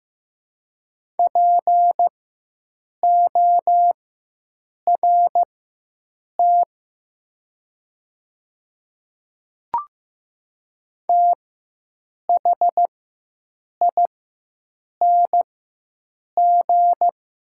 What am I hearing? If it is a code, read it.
Morse code, heard as a keyed tone, about 15 wpm, spaced as for 7 wpm, 700 Hz. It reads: PORT THING
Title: 4x Character Spacing - Top 300 Words - 15wpm